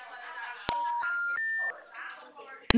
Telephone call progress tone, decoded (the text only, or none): special information tone